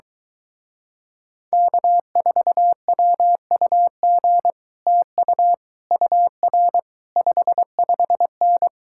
Morse code, heard as a keyed tone, about 24 wpm, 700 Hz. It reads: K4WUG TU UR 55N